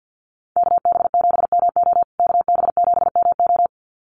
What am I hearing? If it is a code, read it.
Morse code, heard as a keyed tone, about 48 wpm, 700 Hz. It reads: KB7MO KB7MO
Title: USA Call Signs Repeated 2x 50wpm